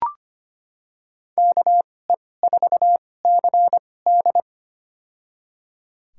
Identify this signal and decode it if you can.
Morse code, heard as a keyed tone, about 25 wpm, spaced as for 18 wpm, 700 Hz. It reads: KE4CD
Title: USA Call Signs - 2x Character Spacing - 25wpm